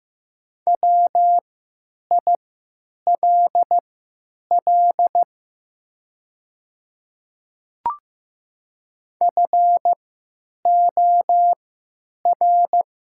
Morse code, heard as a keyed tone, about 15 wpm, spaced as for 9 wpm, 700 Hz. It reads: WILL FOR
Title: 3x Character Spacing - Top 100 Words - 15wpm